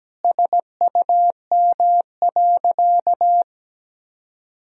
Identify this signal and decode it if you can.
Morse code, heard as a keyed tone, about 17 wpm, 700 Hz. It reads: SUM.
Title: Sentences from Top 100 Words and Abbreviations 17wpm